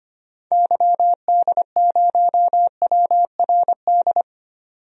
Morse code, heard as a keyed tone, about 25 wpm, 700 Hz. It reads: YD0WRD
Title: DXCC-100 Call Signs - 6 Characters - 25wpm